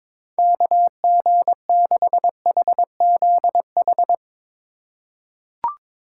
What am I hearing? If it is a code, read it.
Morse code, heard as a keyed tone, about 22 wpm, 700 Hz. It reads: KG6HZH
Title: USA Call Signs - 6 Characters - 2x Character Spacing - 22wpm